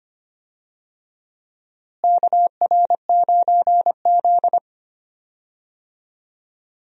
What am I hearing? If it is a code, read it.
Morse code, heard as a keyed tone, about 25 wpm, 700 Hz. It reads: KR9Z
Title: USA Call Signs - 4 Characters - Repeated 3x - 25wpm